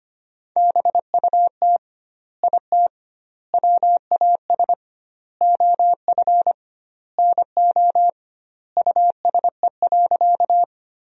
Morse code, heard as a keyed tone, about 25 wpm, 700 Hz. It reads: BUT IT WAS OF NO USE.
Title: 2x Word-Spacing - Sentences from Top 100 Words - 25wpm